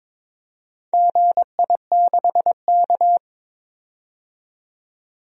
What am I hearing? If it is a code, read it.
Morse code, heard as a keyed tone, about 23 wpm, 700 Hz. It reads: GI6K
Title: DXCC-100 Call Signs - 4 Characters - 22wpm